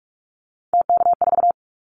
Morse code, heard as a keyed tone, about 46 wpm, 700 Hz. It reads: TK4
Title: Callsign Prefixes for Top 100 DXCC - 45wpm